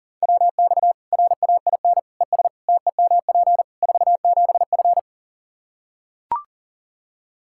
Morse code, heard as a keyed tone, about 40 wpm, 700 Hz. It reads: WX RAIN ES TEMP 47F
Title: Practice QSO - Exchange 2 - Info - 40wpm